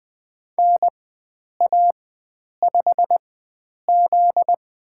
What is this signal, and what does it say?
Morse code, heard as a keyed tone, about 20 wpm, spaced as for 9 wpm, 700 Hz. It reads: NA5Z
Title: USA Call Signs - 4 Characters - 4x Character Spacing - 20wpm